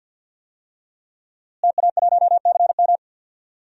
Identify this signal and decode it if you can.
Morse code, heard as a keyed tone, about 47 wpm, 700 Hz. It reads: TA1CM
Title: DXCC-100 Call Signs - 5 Characters - Repeated 2x - 50wpm